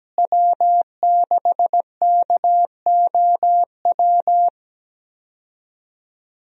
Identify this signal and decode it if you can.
Morse code, heard as a keyed tone, about 17 wpm, 700 Hz. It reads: W6KOW
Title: USA Call Signs - 5 Characters - Repeated 2x - 17wpm